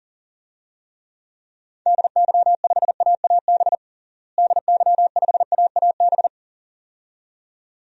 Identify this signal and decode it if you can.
Morse code, heard as a keyed tone, about 37 wpm, 700 Hz. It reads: DY5AAB DY5AAB
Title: DXCC-100 Call Signs - 6 Characters - Repeated 2x - 40wpm